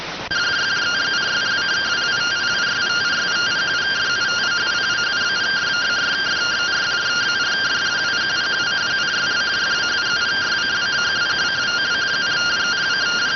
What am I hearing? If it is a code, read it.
Two-tone teleprinter signal, about 170 Hz shift.